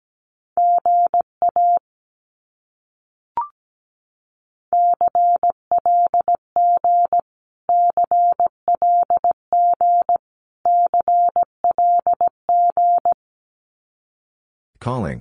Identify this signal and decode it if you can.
Morse code, heard as a keyed tone, about 17 wpm, 700 Hz. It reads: GA CLG CLG CLG
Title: Abbreviations Repeated 3x - 17wpm